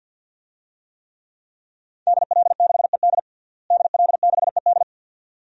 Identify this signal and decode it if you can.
Morse code, heard as a keyed tone, about 52 wpm, 700 Hz. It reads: DL6ED DL6ED
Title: DXCC-100 Call Signs - 5 Characters - Repeated 2x - 50wpm